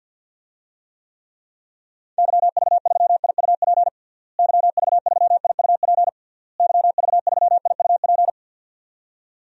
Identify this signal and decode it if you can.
Morse code, heard as a keyed tone, about 46 wpm, 700 Hz. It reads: YV3IUP YV3IUP YV3IUP
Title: DXCC-100 Call Signs - 6 Characters - Repeated 3x - 50wpm